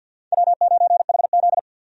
Morse code, heard as a keyed tone, about 52 wpm, 700 Hz. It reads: W9HZ
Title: USA Call Signs - 4 Characters - 50wpm